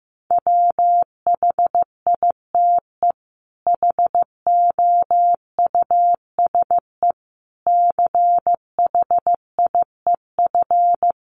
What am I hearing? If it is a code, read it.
Morse code, heard as a keyed tone, about 15 wpm, 700 Hz. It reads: WHITE HOUSE CHIEF
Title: Sets of 5 Words Encore 15wpm